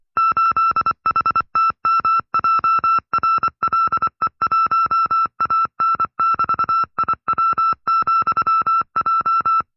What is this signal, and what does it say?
Morse code, heard as a keyed tone, about 24 wpm, 1330 Hz. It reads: 8HTMJRLE1AN<BT>IW,J
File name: FreeFiles / CW-practice / 20WPM-side2